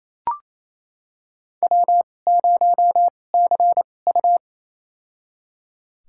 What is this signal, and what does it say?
Morse code, heard as a keyed tone, about 28 wpm, spaced as for 20 wpm, 700 Hz. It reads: W0CU
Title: USA Call Signs - 4 Characters - 2x Character Spacing - 28wpm